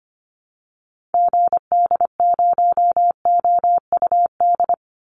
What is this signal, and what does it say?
Morse code, heard as a keyed tone, about 25 wpm, 700 Hz. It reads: GD0OUD
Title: Call Signs Repeated 2x - 25wpm